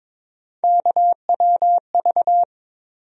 Morse code, heard as a keyed tone, about 22 wpm, 700 Hz. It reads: KWV
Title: USA Call Sign Postfixes - 3 Letters 22wpm